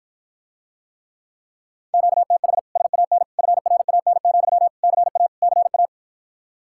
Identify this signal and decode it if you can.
Morse code, heard as a keyed tone, about 54 wpm, 700 Hz. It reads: QTH SAN FRAN, CA CA